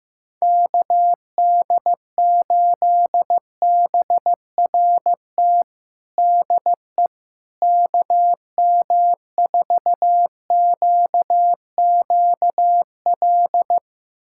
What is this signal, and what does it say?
Morse code, heard as a keyed tone, about 15 wpm, 700 Hz. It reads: KD8BRT DE KM4QQL
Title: USA Callsign DE Callsign 15wpm